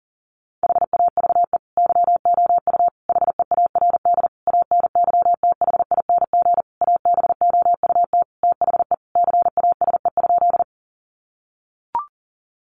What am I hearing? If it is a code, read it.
Morse code, heard as a keyed tone, about 40 wpm, 700 Hz. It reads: HAVE YOU HEARD ANYTHING ABOUT THE CASE?
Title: Sentences from Top 1000 Words 40wpm